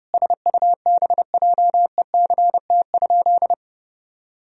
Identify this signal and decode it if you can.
Morse code, heard as a keyed tone, about 30 wpm, 700 Hz. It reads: SUBJECT?
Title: Sentences from Top 1000 Words - Review Part 3 - 30wpm